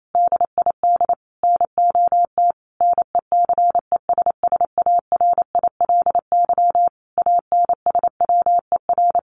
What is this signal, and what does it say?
Morse code, heard as a keyed tone, about 28 wpm, 700 Hz. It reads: DID NOT NECESSARILY ANSWER
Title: Sentences from Top 1400 Words 28wpm